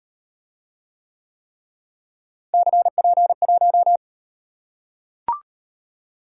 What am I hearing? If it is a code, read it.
Morse code, heard as a keyed tone, about 36 wpm, 700 Hz. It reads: CP1